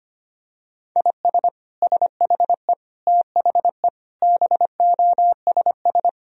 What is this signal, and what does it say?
Morse code, heard as a keyed tone, about 25 wpm, 700 Hz. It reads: IS SHE THE BOSS
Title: Sentences from Top 1000 Words - Review Part 4 - 25wpm